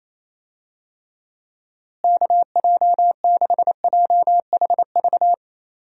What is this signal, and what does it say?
Morse code, heard as a keyed tone, about 28 wpm, 700 Hz. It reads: KJ6JHV
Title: USA Call Signs - 6 Characters - Repeated 2x - 28wpm